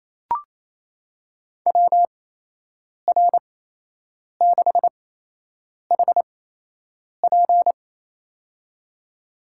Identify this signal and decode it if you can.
Morse code, heard as a keyed tone, about 28 wpm, spaced as for 8 wpm, 700 Hz. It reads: WR6HP
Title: USA Call Signs - 8x Character Spacing - 28wpm